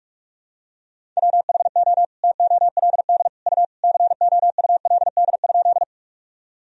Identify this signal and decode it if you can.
Morse code, heard as a keyed tone, about 46 wpm, 700 Hz. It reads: WHO TOLD U COULD?
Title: Sentences from Top 400 Words and Abbreviations 45wpm